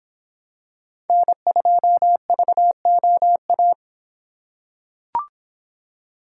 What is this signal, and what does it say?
Morse code, heard as a keyed tone, about 25 wpm, 700 Hz. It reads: N2VOA